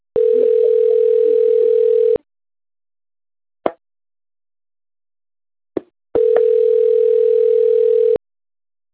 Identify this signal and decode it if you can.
Ringback tone (North America), 440 + 480 Hz.